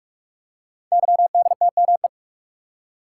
Morse code, heard as a keyed tone, about 44 wpm, 700 Hz. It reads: YDTME
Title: Code Groups - 5 Letters - 45wpm